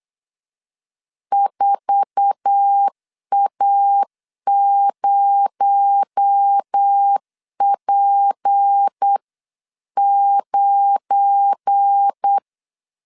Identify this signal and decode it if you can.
Morse code, heard as a keyed tone, about 8 wpm, 800 Hz. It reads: 4A0P9
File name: CW / DET / 1 9 1500C S B 8wpm 5624